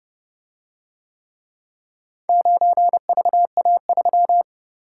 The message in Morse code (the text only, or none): 9VA3